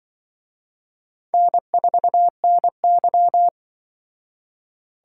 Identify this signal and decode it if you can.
Morse code, heard as a keyed tone, about 24 wpm, 700 Hz. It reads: N4NY
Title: Speed Racing - USA 1x2 Call Signs 20wpm